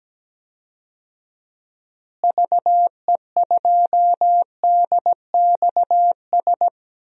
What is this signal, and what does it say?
Morse code, heard as a keyed tone, about 17 wpm, 700 Hz. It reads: VE2DXS
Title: DXCC-100 Call Signs - 6 Characters - Repeated 3x - 17wpm